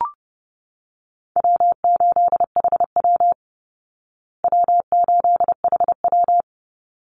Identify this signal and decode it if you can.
Morse code, heard as a keyed tone, about 30 wpm, 700 Hz. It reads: W8HW W8HW
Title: Easy Call Signs Repeated 2x - 30wpm